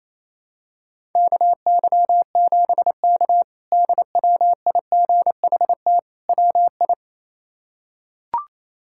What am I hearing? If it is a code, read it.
Morse code, heard as a keyed tone, about 28 wpm, 700 Hz. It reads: KY7K DWIGHT WI
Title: K1USN SST QSO Practice - 28wpm